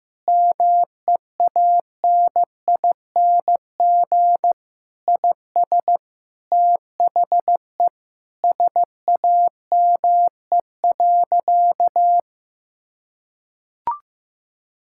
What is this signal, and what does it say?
Morse code, heard as a keyed tone, about 15 wpm, 700 Hz. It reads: MEANING IS THE SAME.
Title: Sentences from Top 1000 Words - Review Part 6 - 15wpm